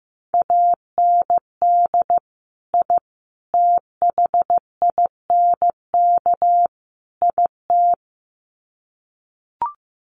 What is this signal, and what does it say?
Morse code, heard as a keyed tone, about 15 wpm, 700 Hz. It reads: AND I THINK IT
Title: Sets of 4 Words 15wpm